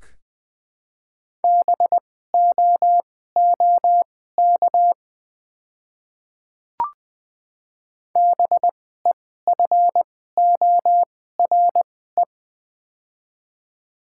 Morse code, heard as a keyed tone, about 20 wpm, spaced as for 14 wpm, 700 Hz. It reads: BOOK BEFORE
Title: 2x Character Spacing - Top 500 Words - 20wpm